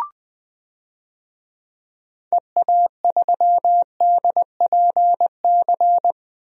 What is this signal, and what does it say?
Morse code, heard as a keyed tone, about 20 wpm, 700 Hz. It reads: EA3DPC
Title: DXCC-100 Call Signs - 6 Characters - Repeated 3x - 20wpm